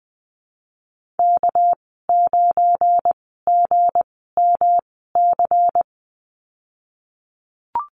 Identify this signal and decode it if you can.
Morse code, heard as a keyed tone, about 20 wpm, spaced as for 15 wpm, 700 Hz. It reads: K9GMC
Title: 245 Lesson 39g - Letter z - Call Signs 20wpm